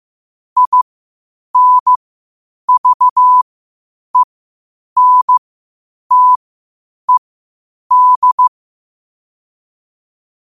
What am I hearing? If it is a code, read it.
Morse code, heard as a keyed tone, about 15 wpm, spaced as for 8 wpm, 1000 Hz. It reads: INVENTED
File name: qotdfiles / 8wpm